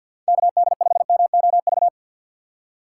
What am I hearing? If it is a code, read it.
Morse code, heard as a keyed tone, about 52 wpm, 700 Hz. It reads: KD5MOV
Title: USA Call Signs - 6 Characters - 3x Character Spacing - 50wpm